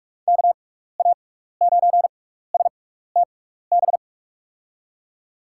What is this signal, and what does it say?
Morse code, heard as a keyed tone, about 45 wpm, spaced as for 15 wpm, 700 Hz. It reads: KA9STB
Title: USA Call Signs - 6 Characters - 6x Character Spacing - 45wpm